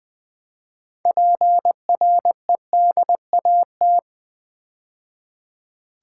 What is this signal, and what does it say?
Morse code, heard as a keyed tone, about 21 wpm, 700 Hz. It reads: PREDAT
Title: Build A Word - 3 Letters At A Time - 8-Letter Words - 20wpm